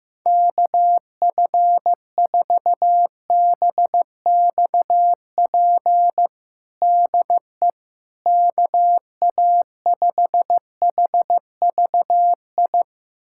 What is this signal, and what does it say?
Morse code, heard as a keyed tone, about 15 wpm, 700 Hz. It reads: KF4BXP DE KA5HVI
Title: USA Callsign DE Callsign - 6 Characters - 15wpm